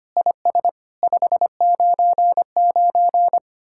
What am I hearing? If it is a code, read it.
Morse code, heard as a keyed tone, about 30 wpm, 700 Hz. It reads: IS 599
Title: Speed-Racing Common QSO Phrases - 25wpm